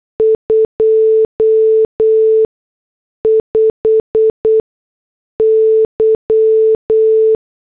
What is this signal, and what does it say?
Morse code, heard as a keyed tone, about 8 wpm, spaced as for 6 wpm, 440 Hz. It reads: 25Y